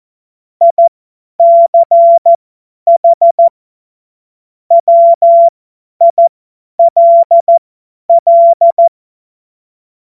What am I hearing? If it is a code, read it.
Morse code, heard as a keyed tone, about 14 wpm, spaced as for 10 wpm, 680 Hz. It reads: ICH WILL
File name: Witz-23-September-2025-50-F70